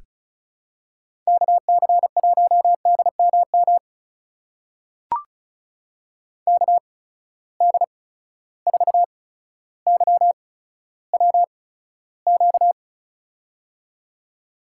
Morse code, heard as a keyed tone, about 36 wpm, spaced as for 15 wpm, 700 Hz. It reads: KC1DMM KD4YWQ
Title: USA Call Signs - 6 Characters - 8x Character Spacing - 35wpm